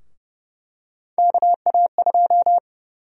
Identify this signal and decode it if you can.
Morse code, heard as a keyed tone, about 30 wpm, 700 Hz. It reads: KA2
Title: USA Call Sign Prefixes - Repeated 2x - 30wpm